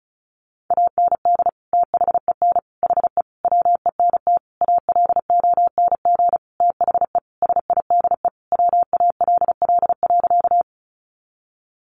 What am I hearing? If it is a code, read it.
Morse code, heard as a keyed tone, about 34 wpm, 700 Hz. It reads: AND THEN HE WENT ALONG THE SIDE WALL.
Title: Sentences from Top 1100 Words 35wpm